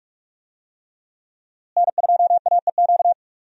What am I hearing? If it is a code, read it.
Morse code, heard as a keyed tone, about 44 wpm, 700 Hz. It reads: N2AEQ
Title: USA Call Signs - 5 Characters - Repeated 3x - 45wpm